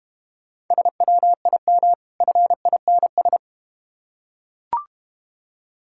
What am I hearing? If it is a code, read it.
Morse code, heard as a keyed tone, about 32 wpm, 690 Hz. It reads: SWIM FINS